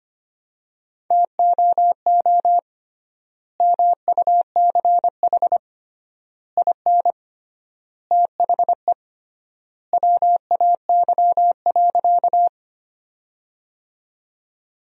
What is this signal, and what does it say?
Morse code, heard as a keyed tone, about 25 wpm, 700 Hz. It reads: TOO MUCH IN THE WAY.
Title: 3x Word-Spacing - Sentences from Top 200 Words - 25wpm